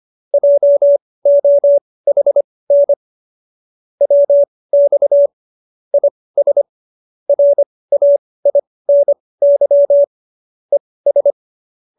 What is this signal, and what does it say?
Morse code, heard as a keyed tone, about 25 wpm, spaced as for 18 wpm, 570 Hz. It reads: JOHN WX IS RAINY ES